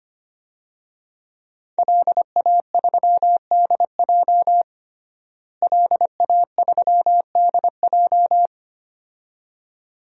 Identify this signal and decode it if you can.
Morse code, heard as a keyed tone, about 25 wpm, 700 Hz. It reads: LA3DJ LA3DJ
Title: DXCC-100 Call Signs - 5 Characters - Repeated 2x - 25wpm